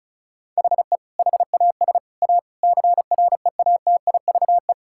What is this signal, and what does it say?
Morse code, heard as a keyed tone, about 36 wpm, 700 Hz. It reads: HE HAS A CREATIVE
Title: Sentences from Top 1000 Words - Review Part 5 - 35wpm